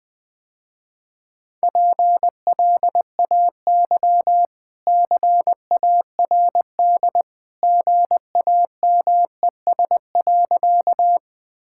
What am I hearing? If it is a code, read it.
Morse code, heard as a keyed tone, about 21 wpm, 700 Hz. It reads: PLAY CARD GAMES.
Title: Easy Sentences with Hints - 20wpm